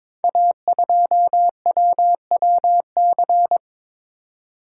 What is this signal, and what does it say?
Morse code, heard as a keyed tone, about 22 wpm, 700 Hz. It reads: A2WWC
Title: Code Groups - 5 Letter:Numbers - 22wpm